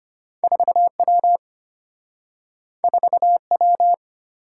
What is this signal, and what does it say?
Morse code, heard as a keyed tone, about 34 wpm, 700 Hz. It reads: 4W 4W